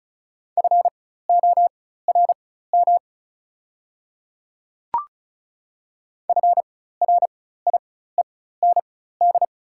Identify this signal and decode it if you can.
Morse code, heard as a keyed tone, about 36 wpm, spaced as for 16 wpm, 700 Hz. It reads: FORM FRIEND
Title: 4x Character Spacing - Top 300 Words - 35wpm